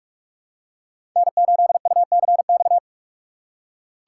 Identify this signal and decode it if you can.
Morse code, heard as a keyed tone, about 44 wpm, 700 Hz. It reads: N8UCX
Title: USA Call Signs - 6x Character Spacing - 45wpm